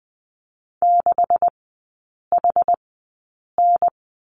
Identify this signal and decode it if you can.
Morse code, heard as a keyed tone, about 20 wpm, spaced as for 8 wpm, 700 Hz. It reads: 6HN
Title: 191 Lesson 31e - Letter v - Three Characters 20wpm